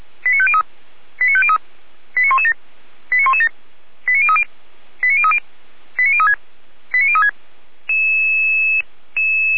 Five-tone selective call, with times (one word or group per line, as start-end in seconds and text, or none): 0.2-0.6 89503
1.2-1.6 89503
2.2-2.5 89117
3.1-3.5 89117
4.1-4.4 89030
5.0-5.4 89030
6.0-6.3 89036
6.9-7.3 89036